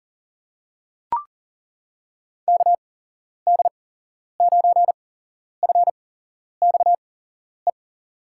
Morse code, heard as a keyed tone, about 38 wpm, spaced as for 11 wpm, 700 Hz. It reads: KD9FXE
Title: USA Call Signs - 6 Characters - 8x Character Spacing - 40wpm